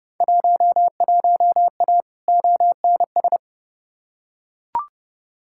Morse code, heard as a keyed tone, about 30 wpm, 700 Hz. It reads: 11A ONS
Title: Field Day Exchanges Encore - 30wpm